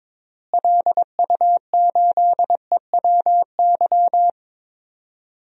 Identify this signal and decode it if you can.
Morse code, heard as a keyed tone, about 22 wpm, 700 Hz. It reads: LU8EWY
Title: DXCC-100 Call Signs - 6 Characters - Repeated 2x - 22wpm